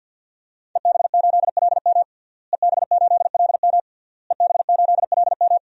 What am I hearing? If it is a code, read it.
Morse code, heard as a keyed tone, about 46 wpm, 700 Hz. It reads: EB8LM EB8LM EB8LM
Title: DXCC-100 Call Signs - 5 Characters - Repeated 3x - 50wpm